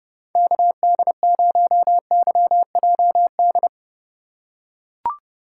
Morse code, heard as a keyed tone, about 32 wpm, 700 Hz. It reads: KD0YJD
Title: USA Call Signs - 2x Character Spacing - 30wpm